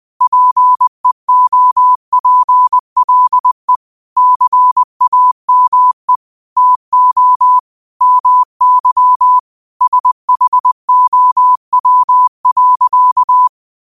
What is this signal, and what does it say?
Morse code, heard as a keyed tone, about 20 wpm, 1000 Hz. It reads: PEOPLE CAME TO MY SHOW.
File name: 20wpm